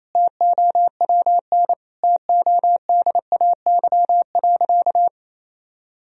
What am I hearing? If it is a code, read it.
Morse code, heard as a keyed tone, about 28 wpm, 700 Hz. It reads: TOWN TODAY.
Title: Sentences from Top 1000 Words - Review Part 3 - 28wpm